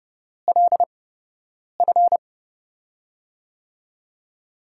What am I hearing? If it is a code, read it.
Morse code, heard as a keyed tone, about 31 wpm, spaced as for 8 wpm, 700 Hz. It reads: LF
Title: USA Call Signs - 6 Characters - 8x Character Spacing - 30wpm